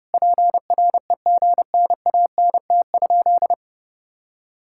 Morse code, heard as a keyed tone, about 30 wpm, 700 Hz. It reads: PREGNANT?
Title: Sentences from Top 1000 Words - Review Part 6 - 30wpm